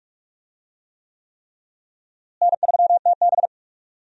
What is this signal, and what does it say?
Morse code, heard as a keyed tone, about 46 wpm, 700 Hz. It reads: N3TB
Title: USA Call Signs - 4 Characters - Repeated 3x - 45wpm